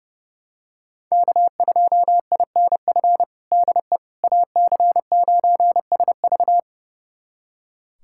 Morse code, heard as a keyed tone, about 32 wpm, 700 Hz. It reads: K2INF DE AC9SV
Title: USA Callsign DE Callsign - 5 Characters - 30wpm